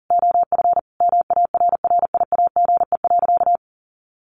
Morse code, heard as a keyed tone, about 40 wpm, 700 Hz. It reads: OF MARRIAGE.
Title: Sentences from Top 800 Words 40wpm